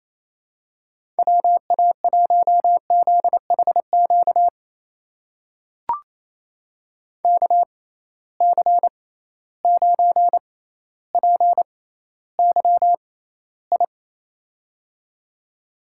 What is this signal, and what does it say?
Morse code, heard as a keyed tone, about 28 wpm, spaced as for 14 wpm, 700 Hz. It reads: WA1ZHQ KC9PYI
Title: USA Call Signs - 6 Characters - 6x Character Spacing - 28wpm